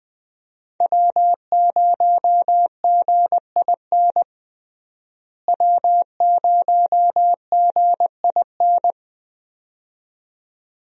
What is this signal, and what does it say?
Morse code, heard as a keyed tone, about 20 wpm, 700 Hz. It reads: W0GIN W0GIN
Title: USA Call Signs - 5 Characters - Repeated 2x - 20wpm